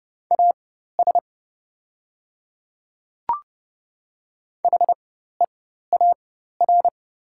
Morse code, heard as a keyed tone, about 30 wpm, spaced as for 14 wpm, 700 Hz. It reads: AS HEAR
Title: 4x Character Spacing - Top 500 Words - 30wpm